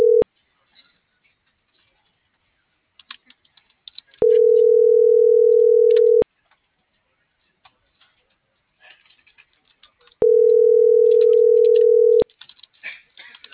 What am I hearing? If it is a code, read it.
Ringback tone (North America), 440 + 480 Hz.